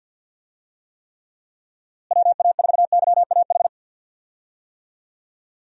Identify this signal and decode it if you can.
Morse code, heard as a keyed tone, about 46 wpm, 700 Hz. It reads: WA4YAH
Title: USA Call Signs - 6 Characters - Repeated 2x - 50wpm